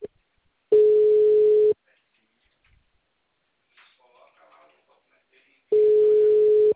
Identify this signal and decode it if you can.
Ringback tone (Europe, 425 Hz), with speech.